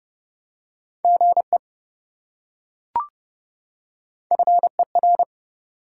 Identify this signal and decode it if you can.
Morse code, heard as a keyed tone, about 27 wpm, 700 Hz. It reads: GE FER